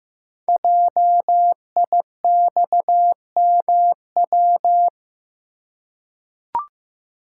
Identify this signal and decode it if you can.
Morse code, heard as a keyed tone, about 15 wpm, 700 Hz. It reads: JIXMW
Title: Code Groups - 5 Letters - 15wpm